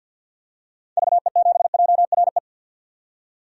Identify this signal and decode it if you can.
Morse code, heard as a keyed tone, about 50 wpm, 700 Hz. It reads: VE7JRE
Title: DXCC-100 Call Signs - 6 Characters - 50wpm